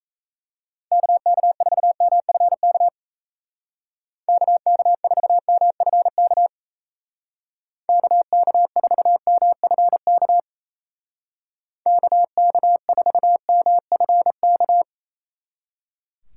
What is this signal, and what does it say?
Morse code, heard as a keyed tone, about 34 wpm, 700 Hz. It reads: KK4MFK KK4MFK KK4MFK KK4MFK
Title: Speed Racing - USA 2x3 Call Signs 28wpm